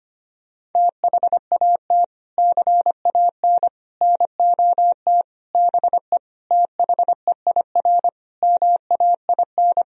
Morse code, heard as a keyed tone, about 25 wpm, 700 Hz. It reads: THAT CAN NOT BE THEIR MAIN
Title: Sentences from Top 1000 Words - Review Part 2 - 25wpm